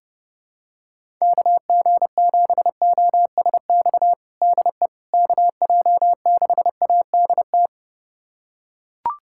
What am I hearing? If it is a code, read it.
Morse code, heard as a keyed tone, about 30 wpm, 700 Hz. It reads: KG7OSX DE KJ6ADT
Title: USA Callsign DE Callsign 30wpm